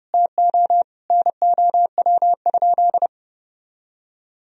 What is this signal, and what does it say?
Morse code, heard as a keyed tone, about 32 wpm, 700 Hz. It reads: TO NOW?
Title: Sentences from Top 100 Words and Abbreviations 30wpm